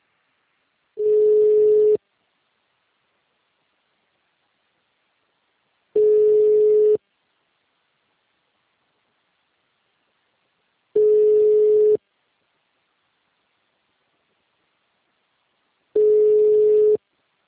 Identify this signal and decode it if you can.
Ringback tone (Europe, 425 Hz).
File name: out- -2031- - - .6